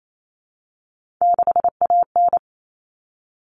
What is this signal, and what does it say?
Morse code, heard as a keyed tone, about 28 wpm, 700 Hz. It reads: 6AN
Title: Common Cut Numbers - 3-Digit 28wpm